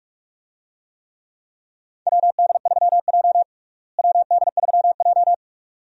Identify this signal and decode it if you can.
Morse code, heard as a keyed tone, about 46 wpm, 700 Hz. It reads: WD3J WD3J
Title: USA Call Signs - 4 Characters - Repeated 2x - 45wpm